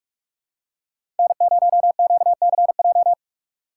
Morse code, heard as a keyed tone, about 46 wpm, 700 Hz. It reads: N0QCJ
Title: USA Call Signs - 5 Characters - 4x Character Spacing - 45wpm